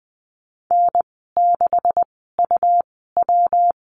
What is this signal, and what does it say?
Morse code, heard as a keyed tone, about 20 wpm, spaced as for 15 wpm, 700 Hz. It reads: N6UW
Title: 179 Lesson 29g - Slash - Call Signs 20wpm